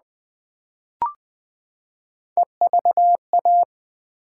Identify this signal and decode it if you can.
Morse code, heard as a keyed tone, about 20 wpm, 700 Hz. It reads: EVA
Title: Build A Word - 3 Letters At A Time - 8-Letter Words - 20wpm